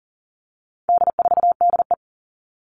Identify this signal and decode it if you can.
Morse code, heard as a keyed tone, about 40 wpm, 700 Hz. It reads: D4DE
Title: All Cut Numbers - 4-Digit 40wpm